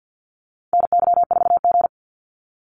Morse code, heard as a keyed tone, about 48 wpm, 700 Hz. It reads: NY4G
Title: NA SOTA Activator Callsigns 50wpm